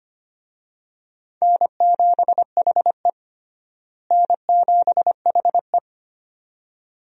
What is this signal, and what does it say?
Morse code, heard as a keyed tone, about 25 wpm, 700 Hz. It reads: N7HE N7HE
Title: USA Call Signs - 4 Characters - Repeated 2x - 25wpm